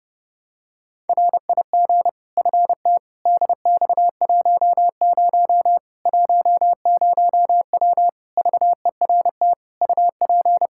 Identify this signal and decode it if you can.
Morse code, heard as a keyed tone, about 30 wpm, 700 Hz. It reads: RIG FT DX10 10W VERT UP
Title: Practice QSO - Exchange 2 - Info - 30wpm